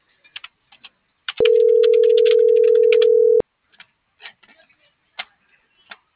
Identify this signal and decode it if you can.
Ringback tone (North America), 440 + 480 Hz.
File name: CENTRAL GENERAL / out eduardo.alatriste